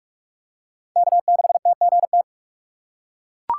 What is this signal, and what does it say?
Morse code, heard as a keyed tone, about 41 wpm, 700 Hz. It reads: K6TGT
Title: USA Call Signs - 5 Characters - 4x Character Spacing - 45wpm